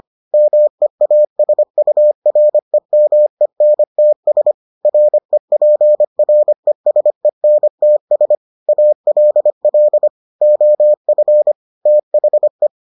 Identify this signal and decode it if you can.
Morse code, heard as a keyed tone, about 25 wpm, 600 Hz. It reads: MEASUREMENTS REPRESENTS ALL OF THE